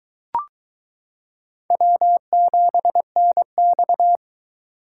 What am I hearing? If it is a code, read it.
Morse code, heard as a keyed tone, about 22 wpm, 700 Hz. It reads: W7NX